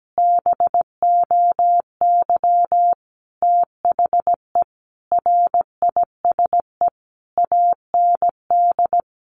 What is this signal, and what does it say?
Morse code, heard as a keyed tone, about 17 wpm, 700 Hz. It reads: BOY THE RISE AND
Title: Sentences from Top 1500 Words 17wpm